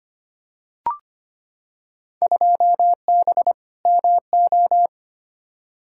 Morse code, heard as a keyed tone, about 25 wpm, 700 Hz. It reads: 2B MO
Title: Field Day Exchanges Encore - 25wpm